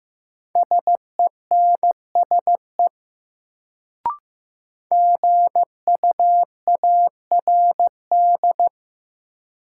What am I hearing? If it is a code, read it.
Morse code, heard as a keyed tone, about 15 wpm, 700 Hz. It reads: SENSE GUARD
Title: Sentences from Top 700 Words - New Word Drill - 15wpm